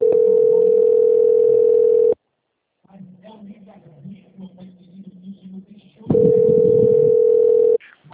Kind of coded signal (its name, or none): telephone call progress tone